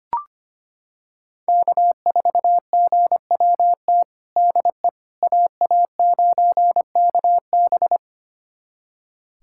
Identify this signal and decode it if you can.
Morse code, heard as a keyed tone, about 25 wpm, 700 Hz. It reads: K4GWT DE AA9KB
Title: USA Callsign DE Callsign 25wpm